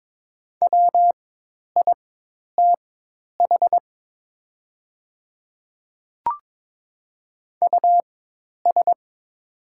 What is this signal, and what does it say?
Morse code, heard as a keyed tone, about 22 wpm, spaced as for 10 wpm, 700 Hz. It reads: WITH US
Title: 4x Character Spacing - Top 300 Words - 22wpm